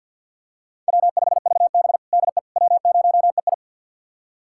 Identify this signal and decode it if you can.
Morse code, heard as a keyed tone, about 48 wpm, 700 Hz. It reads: W5VB DE W0EI
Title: USA Callsign DE Callsign - 4 Characters - 50wpm